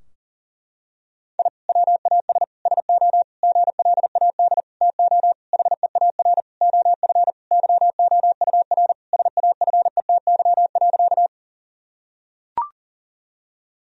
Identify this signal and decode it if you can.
Morse code, heard as a keyed tone, about 37 wpm, 700 Hz. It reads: I WAS SO GLAD TO HEAR OF YOUR SAFETY.
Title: Sentences from Top 1000 Words - Review Part 4 - 40wpm